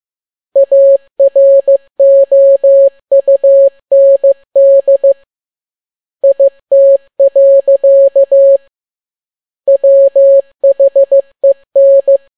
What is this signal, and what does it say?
Morse code, heard as a keyed tone, about 15 wpm, 550 Hz. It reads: AROUND IT. WHEN
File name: n6tt p1 15C13